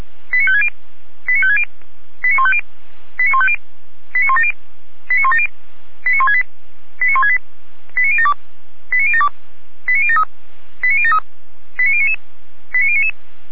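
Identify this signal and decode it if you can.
Five-tone selective call (ZVEI), heard as quick runs of five tones, tones of about 70 ms. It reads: 89560 89560 89150 89150 89170 89170 89168 89168 89072 89072 89073 89073 89099 89099
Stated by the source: ZVEI five-tone call 89167 received on SDR, from audio.f85360.feuerwehr-orlamuende.org (fire brigade Orlamünde).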